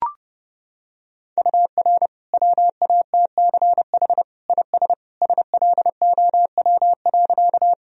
Morse code, heard as a keyed tone, about 32 wpm, 700 Hz. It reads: UR WATCH IS SLOW.
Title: Sentences from Top 400 Words and Abbreviations 30wpm